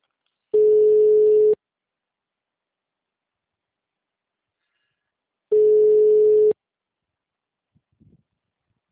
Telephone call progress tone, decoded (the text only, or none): ringback tone